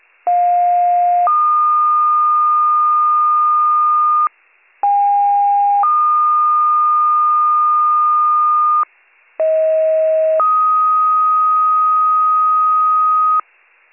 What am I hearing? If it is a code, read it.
Two-tone page, heard as a one-second tone, then a three-second tone, about 710 then 1220 Hz, about 810 then 1220 Hz, about 640 then 1250 Hz. It is heard from 0.3 to 13.4 s.